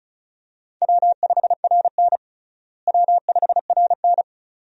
Morse code, heard as a keyed tone, about 34 wpm, 700 Hz. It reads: W5RN W5RN
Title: USA Call Signs - 4 Characters - Repeated 2x - 35wpm